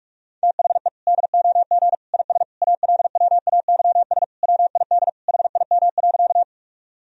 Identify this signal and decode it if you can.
Morse code, heard as a keyed tone, about 42 wpm, 700 Hz. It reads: THE DOG IS ALWAYS WID HIM.
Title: Sentences from Top 400 Words and Abbreviations 45wpm